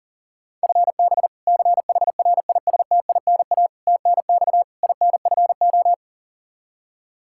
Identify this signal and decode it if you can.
Morse code, heard as a keyed tone, about 40 wpm, 700 Hz. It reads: FB CHRISTINA TNX INFO
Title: Practice QSO - Exchange 3 - More Info - 40wpm